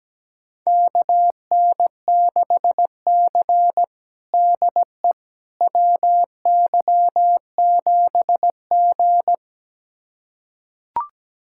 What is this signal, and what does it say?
Morse code, heard as a keyed tone, about 17 wpm, 700 Hz. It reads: KN6C DE WY7G
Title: USA Callsign DE Callsign - 4 Characters - 17wpm